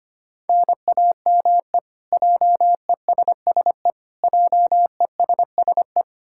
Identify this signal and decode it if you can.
Morse code, heard as a keyed tone, about 25 wpm, 700 Hz. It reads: NAME JESSE JESSE
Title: K1USN SST QSO Practice 25wpm